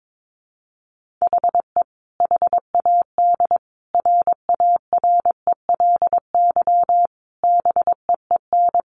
Morse code, heard as a keyed tone, about 22 wpm, 700 Hz. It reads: HE HAD RARELY BEEN
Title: Sentences from Top 1400 Words 22wpm